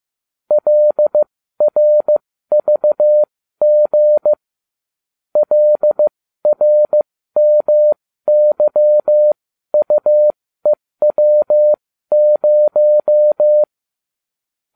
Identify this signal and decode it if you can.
Morse code, heard as a keyed tone, about 15 wpm, spaced as for 13 wpm, 600 Hz. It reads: LRVG LRMYUEW0